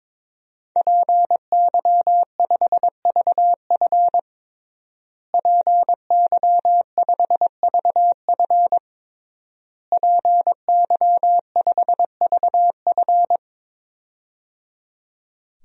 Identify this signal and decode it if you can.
Morse code, heard as a keyed tone, about 22 wpm, 700 Hz. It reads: PY5VF PY5VF PY5VF
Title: DXCC-100 Call Signs - 5 Characters - Repeated 3x - 22wpm